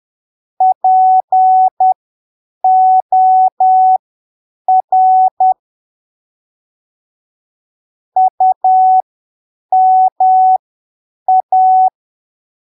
Morse code, heard as a keyed tone, about 10 wpm, spaced as for 7 wpm, 750 Hz. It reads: POR UMA
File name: arquivos / texto / texto 11 10ppm